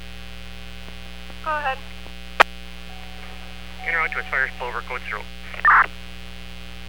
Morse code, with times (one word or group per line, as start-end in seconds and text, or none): none